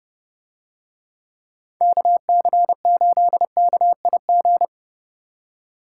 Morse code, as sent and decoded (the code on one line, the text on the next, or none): -.- -.-. ---.. -.- .. --.
KC8KIG